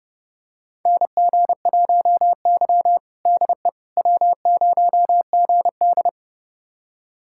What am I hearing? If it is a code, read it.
Morse code, heard as a keyed tone, about 32 wpm, 700 Hz. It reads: NG1Y DE W0GD
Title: USA Callsign DE Callsign - 4 Characters - 30wpm